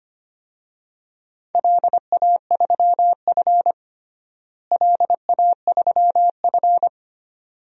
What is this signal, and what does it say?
Morse code, heard as a keyed tone, about 25 wpm, 700 Hz. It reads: LA3F LA3F
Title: DXCC-100 Call Signs - 4 Characters - Repeated 2x - 25wpm